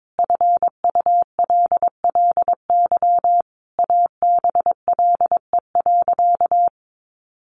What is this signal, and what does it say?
Morse code, heard as a keyed tone, about 22 wpm, 700 Hz. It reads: FULLY ABLE.
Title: Sentences from Top 1300 Words 22wpm